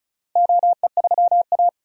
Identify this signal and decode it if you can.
Morse code, heard as a keyed tone, about 36 wpm, 700 Hz. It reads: OE3A
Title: DXCC-100 Call Signs - 4 Characters - Repeated 2x - 35wpm